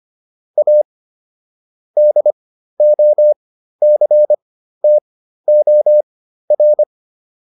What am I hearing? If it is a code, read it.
Morse code, heard as a keyed tone, about 25 wpm, spaced as for 13 wpm, 600 Hz. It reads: A DOCTOR